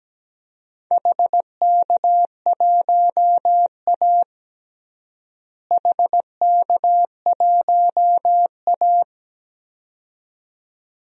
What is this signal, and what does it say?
Morse code, heard as a keyed tone, about 17 wpm, 700 Hz. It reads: HK1A HK1A
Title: DXCC-100 Call Signs - 4 Characters - Repeated 2x - 17wpm